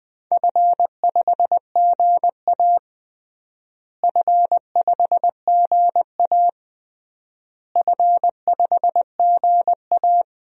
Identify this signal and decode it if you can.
Morse code, heard as a keyed tone, about 20 wpm, 700 Hz. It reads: F5GA F5GA F5GA
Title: DXCC-100 Call Signs - 4 Characters - Repeated 3x - 20wpm